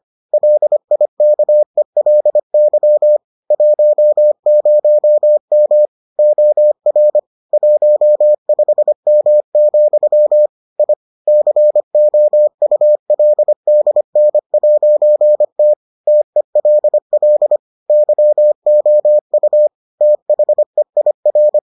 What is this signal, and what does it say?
Morse code, heard as a keyed tone, about 25 wpm, 600 Hz. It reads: LIKELY 10M OR 15M, I COULDN'T TELL YOU THEIR